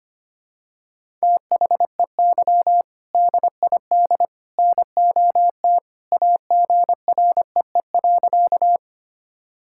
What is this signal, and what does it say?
Morse code, heard as a keyed tone, about 25 wpm, 700 Hz. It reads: THEY DID NOT AGREE.
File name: practice / 25wpm / Sentences with 3-Word Phrases - 25wpm